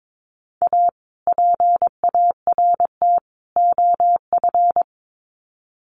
Morse code, heard as a keyed tone, about 22 wpm, 700 Hz. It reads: A PART OF
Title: Sets of 3 Words 22wpm